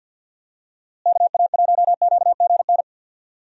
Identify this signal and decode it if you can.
Morse code, heard as a keyed tone, about 52 wpm, 700 Hz. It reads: KA1QGN
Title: USA Call Signs - 6 Characters - Repeated 3x - 50wpm